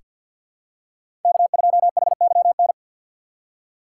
Morse code, heard as a keyed tone, about 47 wpm, 700 Hz. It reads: K2HYN
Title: USA Call Signs - 2x Character Spacing - 50wpm